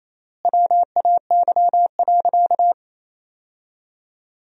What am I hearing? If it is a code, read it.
Morse code, heard as a keyed tone, about 28 wpm, 700 Hz. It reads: WAY.
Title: 3x Word-Spacing - Sentences from Top 100 Words - 28wpm